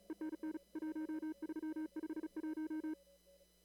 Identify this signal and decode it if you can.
Morse code, heard as a keyed tone, about 35 wpm, 310 Hz. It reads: ENN 1351
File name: CW / kt5j wpx cw-op-k5pi